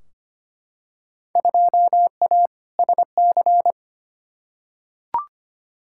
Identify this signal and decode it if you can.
Morse code, heard as a keyed tone, about 25 wpm, 700 Hz. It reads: 2A SC
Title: Field Day Exchanges - 25wpm